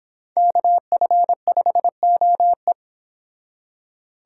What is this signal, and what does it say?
Morse code, heard as a keyed tone, about 32 wpm, 700 Hz. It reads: KF5OE